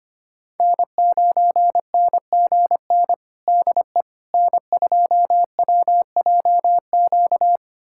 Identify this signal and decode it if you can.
Morse code, heard as a keyed tone, about 25 wpm, 700 Hz. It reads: N9NGN DE N2WJQ
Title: USA Callsign DE Callsign 25wpm